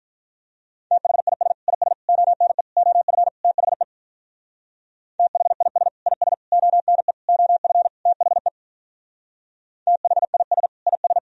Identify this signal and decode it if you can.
Morse code, heard as a keyed tone, about 43 wpm, 700 Hz. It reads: THIS IS ONE OF THE THIS IS ONE OF THE THIS IS
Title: Speed-Racing - Sets of 5 Words 35wpm